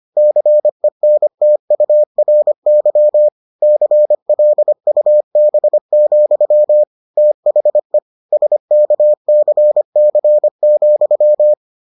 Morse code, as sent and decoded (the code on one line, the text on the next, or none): -.-. -. - ..- .-. -.-- / -.-. .-.. ..- -... --..-- / - .... / ... -.- -.-. -.-. --..--
CENTURY CLUB, THE SKCC,